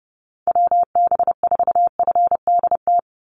Morse code, heard as a keyed tone, about 30 wpm, 700 Hz. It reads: WB4FDT
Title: Call Signs Repeated 3x - 30wpm